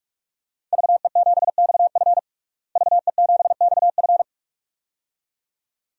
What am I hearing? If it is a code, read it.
Morse code, heard as a keyed tone, about 46 wpm, 700 Hz. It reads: VE7XF VE7XF
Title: DXCC-100 Call Signs - 5 Characters - Repeated 2x - 45wpm